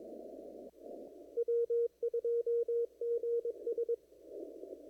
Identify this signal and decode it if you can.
Morse code, heard as a keyed tone, about 22 wpm, 470 Hz. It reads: W2GS
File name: CW / kt5j wpx cw-op-k5pi